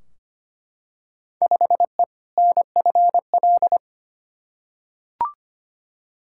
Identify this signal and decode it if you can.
Morse code, heard as a keyed tone, about 25 wpm, 700 Hz. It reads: 5E NFL
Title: Field Day Exchanges Encore - 25wpm